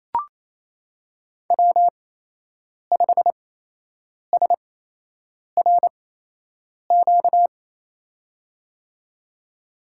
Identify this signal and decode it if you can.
Morse code, heard as a keyed tone, about 28 wpm, spaced as for 8 wpm, 700 Hz. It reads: W5SRQ
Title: USA Call Signs - 5 Characters - 8x Character Spacing - 28wpm